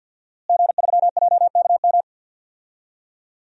Morse code, heard as a keyed tone, about 47 wpm, 700 Hz. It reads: G3JKM
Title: DXCC-100 Call Signs - 5 Characters - Repeated 2x - 50wpm